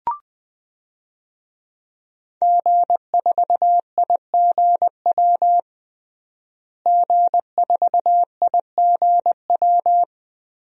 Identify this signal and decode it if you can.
Morse code, heard as a keyed tone, about 20 wpm, 700 Hz. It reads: G4IGW G4IGW
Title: DXCC-100 Call Signs - 5 Characters - Repeated 2x - 20wpm